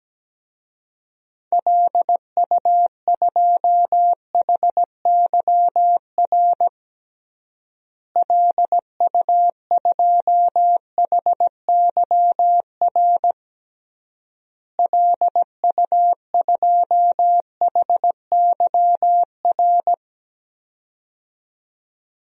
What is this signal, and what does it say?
Morse code, heard as a keyed tone, about 17 wpm, 700 Hz. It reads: LU2HYR LU2HYR LU2HYR
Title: DXCC-100 Call Signs - 6 Characters - Repeated 3x - 17wpm